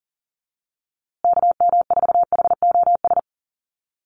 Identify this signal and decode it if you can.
Morse code, heard as a keyed tone, about 40 wpm, 700 Hz. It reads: KM4HOS
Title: USA Call Signs Repeated 3x 40wpm